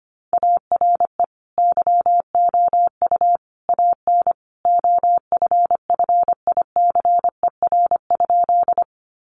Morse code, heard as a keyed tone, about 25 wpm, 700 Hz. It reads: ARE YOU AN OFFICER?
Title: Sentences from Top 1200 Words 25wpm